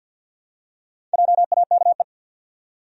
Morse code, heard as a keyed tone, about 50 wpm, 700 Hz. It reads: JAKE
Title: USA Common Names - Max 5 Letters - 50wpm